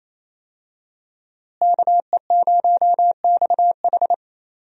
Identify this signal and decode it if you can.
Morse code, heard as a keyed tone, about 28 wpm, 700 Hz. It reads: KE0XH
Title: USA Call Signs - 5 Characters - Repeated 2x - 28wpm